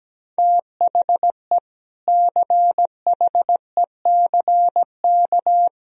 Morse code, heard as a keyed tone, about 17 wpm, 700 Hz. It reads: THE CHECK
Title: Sentences from Top 1000 Words - Review Part 3 - 17wpm